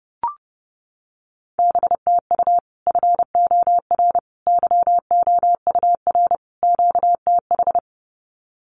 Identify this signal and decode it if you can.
Morse code, heard as a keyed tone, about 30 wpm, 700 Hz. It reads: B TU FOR YOUR QTH
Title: Common QSO Phrases 30wpm